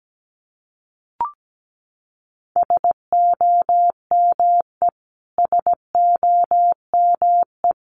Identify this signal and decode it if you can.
Morse code, heard as a keyed tone, about 17 wpm, 700 Hz. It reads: SOME SOME
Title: Top 500 Words Repeated 17wpm